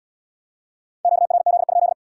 Morse code, heard as a keyed tone, about 69 wpm, 700 Hz. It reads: QRZ?